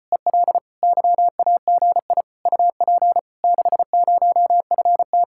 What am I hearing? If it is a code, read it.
Morse code, heard as a keyed tone, about 33 wpm, 700 Hz. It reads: EL YAGI UP 60FT